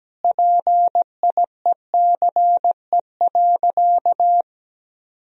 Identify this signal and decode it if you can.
Morse code, heard as a keyed tone, about 17 wpm, 700 Hz. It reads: PIECE.
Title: Sentences from Top 1000 Words - Review Part 3 - 17wpm